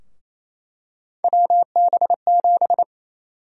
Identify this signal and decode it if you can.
Morse code, heard as a keyed tone, about 27 wpm, 700 Hz. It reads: WB7
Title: USA Call Sign Prefixes - 28wpm